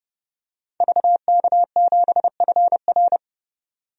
Morse code, heard as a keyed tone, about 30 wpm, 700 Hz. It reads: VK7FR
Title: DXCC-100 Call Signs - 5 Characters - 30wpm